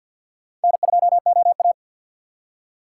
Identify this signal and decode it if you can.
Morse code, heard as a keyed tone, about 62 wpm, 700 Hz. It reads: N2OA
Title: Speed Racing - USA 1x2 Call Signs 50wpm